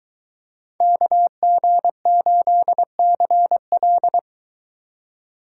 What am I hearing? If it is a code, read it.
Morse code, heard as a keyed tone, about 23 wpm, 700 Hz. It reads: KG8CL